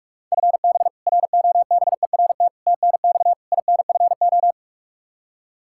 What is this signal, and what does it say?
Morse code, heard as a keyed tone, about 41 wpm, 700 Hz. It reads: FB ROBERT TNX INFO
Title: K1USN SST QSO Practice 45wpm